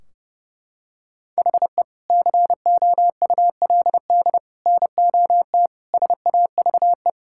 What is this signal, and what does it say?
Morse code, heard as a keyed tone, about 30 wpm, 700 Hz. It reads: HE COULD NOT SAVE
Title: Sentences from Top 1600 Words 30wpm